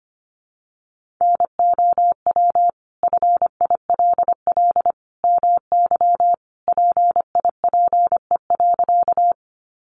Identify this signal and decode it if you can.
Morse code, heard as a keyed tone, about 25 wpm, 700 Hz. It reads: NOW FILL MY PIPE.
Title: Sentences from Top 1500 Words 25wpm